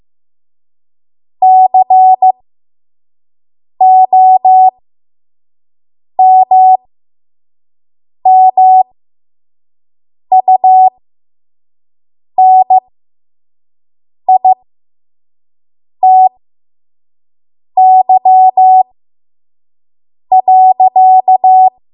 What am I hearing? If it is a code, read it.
Morse code, heard as a keyed tone, about 15 wpm, spaced as for 5 wpm, 750 Hz. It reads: COMMUNITY.
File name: Morse / 05WPM